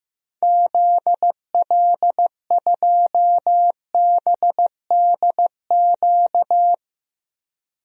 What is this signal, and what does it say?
Morse code, heard as a keyed tone, about 15 wpm, 700 Hz. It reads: ZL2BDQ
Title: DXCC-100 Call Signs - 6 Characters - Repeated 2x - 15wpm